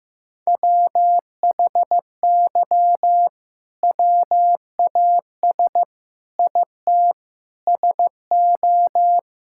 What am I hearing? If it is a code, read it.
Morse code, heard as a keyed tone, about 15 wpm, 700 Hz. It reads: WHY WAS IT SO
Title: Sentences from Top 1000 Words - Review Part 5 - 15wpm